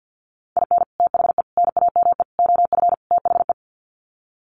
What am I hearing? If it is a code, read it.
Morse code, heard as a keyed tone, about 49 wpm, 700 Hz. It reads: IN THE NAME OF THE
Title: Sets of 5 Words Encore 50wpm